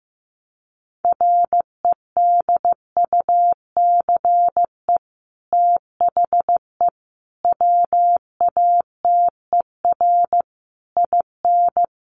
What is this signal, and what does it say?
Morse code, heard as a keyed tone, about 15 wpm, 700 Hz. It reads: REDUCE THE WATER IN
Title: Sentences from Top 1100 Words 15wpm